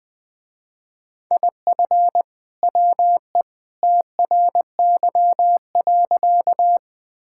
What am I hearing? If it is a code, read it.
Morse code, heard as a keyed tone, about 20 wpm, 700 Hz. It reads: IF WE TRY.
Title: Sentences with 2-Word Phrases - Spoken Beginning - 20wpm